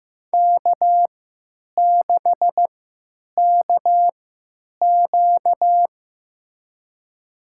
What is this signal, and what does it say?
Morse code, heard as a keyed tone, about 15 wpm, spaced as for 9 wpm, 700 Hz. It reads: K6KQ
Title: USA Call Signs - 4 Characters - 3x Character Spacing - 15wpm